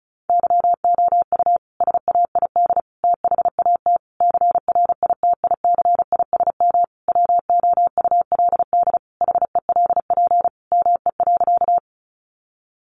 Morse code, heard as a keyed tone, about 34 wpm, 700 Hz. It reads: YOU SAID THAT CRITICISM WOULD HELP ME.
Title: Sentences from Top 1300 Words 35wpm